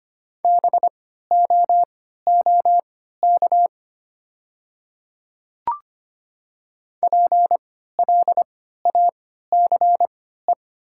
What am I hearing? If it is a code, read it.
Morse code, heard as a keyed tone, about 25 wpm, spaced as for 14 wpm, 700 Hz. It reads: BOOK PLACE
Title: 3x Character Spacing - Top 500 Words - 25wpm